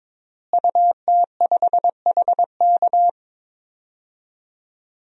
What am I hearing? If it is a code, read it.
Morse code, heard as a keyed tone, about 22 wpm, 700 Hz. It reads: UT5HK
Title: DXCC-100 Call Signs - 5 Characters - Repeated 2x - 22wpm